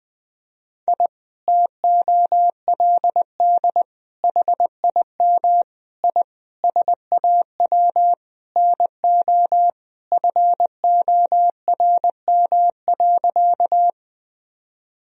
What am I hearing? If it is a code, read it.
Morse code, heard as a keyed tone, about 20 wpm, 700 Hz. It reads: I TOLD HIM I SAW NO FORM.
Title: Sentences from Top 1000 Words - Review Part 1 - 20wpm